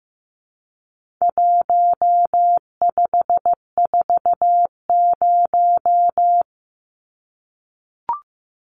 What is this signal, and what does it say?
Morse code, heard as a keyed tone, about 15 wpm, 700 Hz. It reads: 1540
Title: Numbers - 4-Digit 15wpm